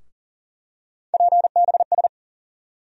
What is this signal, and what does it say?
Morse code, heard as a keyed tone, about 40 wpm, 700 Hz. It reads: PBS
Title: USA Call Sign Postfixes - 3 Letters 40wpm